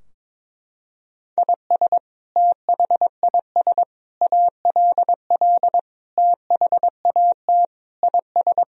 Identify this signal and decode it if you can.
Morse code, heard as a keyed tone, about 22 wpm, 700 Hz. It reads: IS THIS ALL THAT IS
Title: Sentences from Top 1000 Words - Review Part 3 - 22wpm